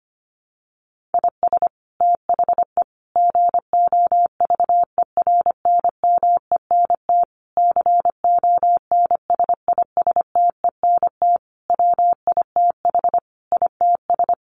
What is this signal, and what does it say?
Morse code, heard as a keyed tone, about 25 wpm, 700 Hz. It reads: IS THE GOVERNMENT CONSISTENT WITH ITS
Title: Sentences from Top 1500 Words 25wpm